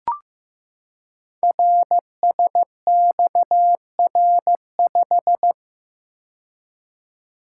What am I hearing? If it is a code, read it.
Morse code, heard as a keyed tone, about 15 wpm, 700 Hz. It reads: RSXR5
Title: Code Groups - 5 Letter:Numbers - 15wpm